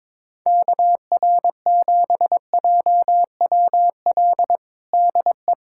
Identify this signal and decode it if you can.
Morse code, heard as a keyed tone, about 22 wpm, 700 Hz. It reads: KR7JWL DE
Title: USA Callsign DE Callsign - 6 Characters - 22wpm